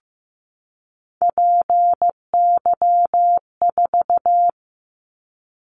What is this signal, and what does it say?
Morse code, heard as a keyed tone, about 15 wpm, 700 Hz. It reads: PY4
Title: Callsign Prefixes for Top 50 DXCC - 15wpm